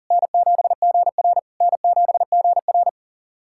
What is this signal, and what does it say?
Morse code, heard as a keyed tone, about 40 wpm, 690 Hz. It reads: N7GR N7GR
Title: SOTA QSO Practice 40wpm